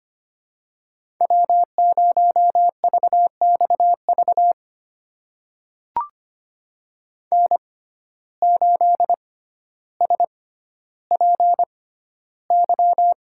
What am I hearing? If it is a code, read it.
Morse code, heard as a keyed tone, about 25 wpm, spaced as for 13 wpm, 700 Hz. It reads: W0VXV N8SPY
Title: USA Call Signs - 5 Characters - 6x Character Spacing - 25wpm